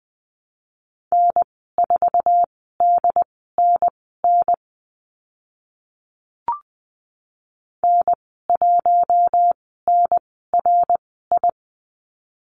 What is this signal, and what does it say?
Morse code, heard as a keyed tone, about 20 wpm, spaced as for 15 wpm, 700 Hz. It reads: N4DNN N1NRI
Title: 069 Lesson 12g - Letter d - Call Signs 20wpm